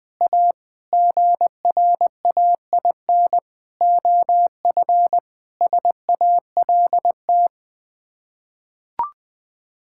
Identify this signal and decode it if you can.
Morse code, heard as a keyed tone, about 21 wpm, 700 Hz. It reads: A GRAIN OF SALT
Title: Sets of 8 Words - 20wpm